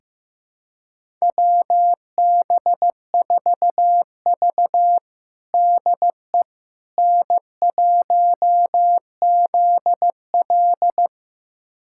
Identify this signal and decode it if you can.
Morse code, heard as a keyed tone, about 15 wpm, 700 Hz. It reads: WB4V DE N1ZL
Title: USA Callsign DE Callsign - 4 Characters - 15wpm